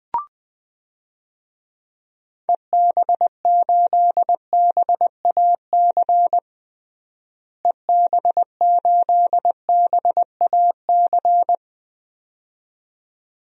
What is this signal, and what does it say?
Morse code, heard as a keyed tone, about 21 wpm, 700 Hz. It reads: EB8BAC EB8BAC
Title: DXCC-100 Call Signs - 6 Characters - Repeated 2x - 20wpm